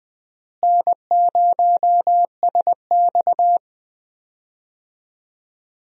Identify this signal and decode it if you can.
Morse code, heard as a keyed tone, about 20 wpm, 700 Hz. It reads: N0SX
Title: USA Call Signs - 4 Characters - Repeated 3x - 20wpm